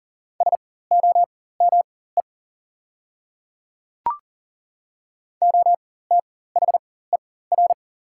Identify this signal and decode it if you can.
Morse code, heard as a keyed tone, about 40 wpm, spaced as for 19 wpm, 690 Hz. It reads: SOME OTHER
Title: 4x Character Spacing - Top 100 Words - 40wpm